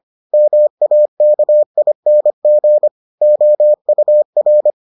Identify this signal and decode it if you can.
Morse code, heard as a keyed tone, about 25 wpm, 600 Hz. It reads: MAKING OUR